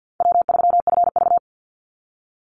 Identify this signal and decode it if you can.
Morse code, heard as a keyed tone, about 50 wpm, 700 Hz. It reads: W3FV
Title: Easy Call Signs - 50wpm